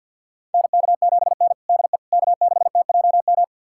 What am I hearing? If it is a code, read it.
Morse code, heard as a keyed tone, about 52 wpm, 700 Hz. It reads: NK7N DE K6TJM
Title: USA Callsign DE Callsign 50wpm